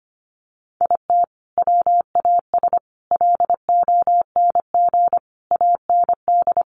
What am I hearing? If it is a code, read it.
Morse code, heard as a keyed tone, about 25 wpm, 700 Hz. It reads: IT WAS LONG AND
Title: Sentences from Top 1400 Words 25wpm